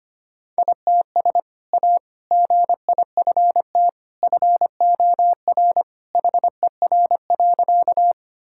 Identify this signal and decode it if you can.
Morse code, heard as a keyed tone, about 25 wpm, 700 Hz. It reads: ITS A GIFT FOR HER.
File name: practice / 25wpm / Sentences with 2-Word Phrases - 25wpm